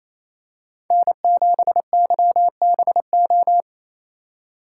Morse code, heard as a keyed tone, about 28 wpm, 700 Hz. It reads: N7YBO
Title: USA Call Signs - 5 Characters - 2x Character Spacing - 28wpm